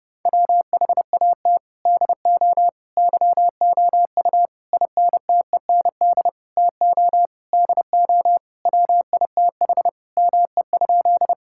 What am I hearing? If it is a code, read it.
Morse code, heard as a keyed tone, about 30 wpm, 700 Hz. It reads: WHAT DO YOU INTEND TO DO WITH ME?
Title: Sentences from Top 1600 Words 30wpm